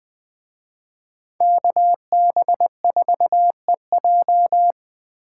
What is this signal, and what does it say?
Morse code, heard as a keyed tone, about 21 wpm, 700 Hz. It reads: KB4EJ